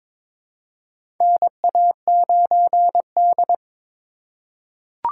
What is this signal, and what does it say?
Morse code, heard as a keyed tone, about 22 wpm, 700 Hz. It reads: NA9D
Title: USA Call Signs - 4 Characters - 2x Character Spacing - 22wpm